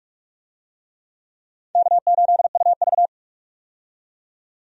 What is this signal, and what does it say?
Morse code, heard as a keyed tone, about 42 wpm, 700 Hz. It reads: K8UV
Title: USA Call Signs - 4 Characters - 45wpm